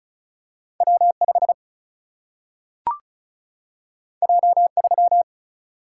Morse code, heard as a keyed tone, about 36 wpm, 700 Hz. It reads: W 5 J 3
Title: Single-Letter Single-Number 35wpm